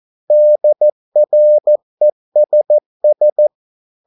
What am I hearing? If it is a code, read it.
Morse code, heard as a keyed tone, about 14 wpm, 600 Hz. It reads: DRESS